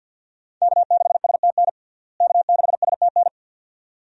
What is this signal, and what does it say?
Morse code, heard as a keyed tone, about 52 wpm, 700 Hz. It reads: K6STN K6STN
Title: USA Call Signs - 5 Characters - Repeated 2x - 50wpm